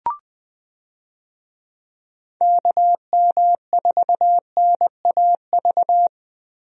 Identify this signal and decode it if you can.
Morse code, heard as a keyed tone, about 20 wpm, 700 Hz. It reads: KM4NAV
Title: USA Call Signs - 6 Characters - Repeated 3x - 20wpm